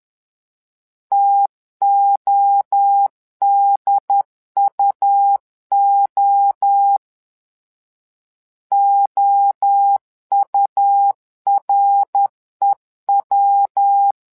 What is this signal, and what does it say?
Morse code, heard as a keyed tone, about 10 wpm, 800 Hz. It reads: TODUO OUREW
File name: LEZIONI / 3 MWRO / 3 MWRO 10wpm 2148